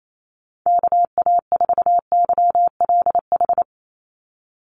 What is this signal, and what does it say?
Morse code, heard as a keyed tone, about 28 wpm, 700 Hz. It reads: KA4YLH
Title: USA Call Signs 28wpm